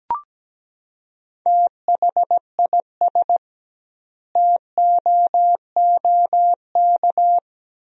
Morse code, heard as a keyed tone, about 17 wpm, 700 Hz. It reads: THIS TOOK
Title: 2x Word-Spacing - Sentences from Top 300 Words - 17wpm